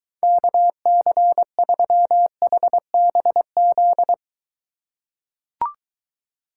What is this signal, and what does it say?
Morse code, heard as a keyed tone, about 22 wpm, 700 Hz. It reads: KC3HBZ